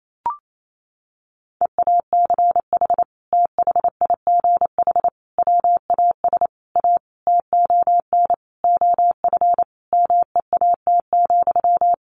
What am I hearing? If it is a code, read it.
Morse code, heard as a keyed tone, about 28 wpm, 700 Hz. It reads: EACH THIGH WAS A TON OF MEAT,
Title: A Sound Of Thunder 28wpm